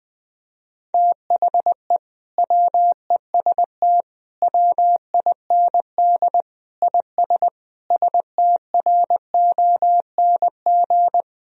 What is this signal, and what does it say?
Morse code, heard as a keyed tone, about 21 wpm, 700 Hz. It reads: THE WEST WIND IS STRONG
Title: Sentences from Top 1000 Words - Review Part 2 - 20wpm